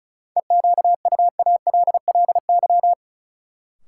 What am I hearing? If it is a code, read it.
Morse code, heard as a keyed tone, about 34 wpm, 700 Hz. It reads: EQUALLY
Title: Sentences from Top 1400 Words - New Word Drill Alt - 35wpm